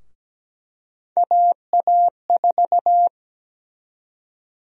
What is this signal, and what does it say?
Morse code, heard as a keyed tone, about 17 wpm, 700 Hz. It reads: AA4
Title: USA Call Sign Prefixes - 17wpm